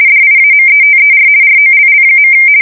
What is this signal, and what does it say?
RTTY, about 50 baud. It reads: DIAMETER ROLL.
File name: fsk170-50B / teletype-corp